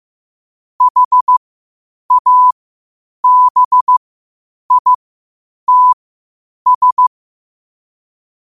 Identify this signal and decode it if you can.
Morse code, heard as a keyed tone, about 15 wpm, spaced as for 8 wpm, 1000 Hz. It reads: HABITS
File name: qotdfiles / 8wpm